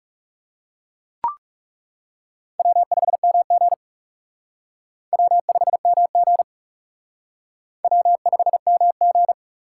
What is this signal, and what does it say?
Morse code, heard as a keyed tone, about 36 wpm, 700 Hz. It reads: W5MG W5MG W5MG
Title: Speed Racing - USA 1x2 Call Signs 30wpm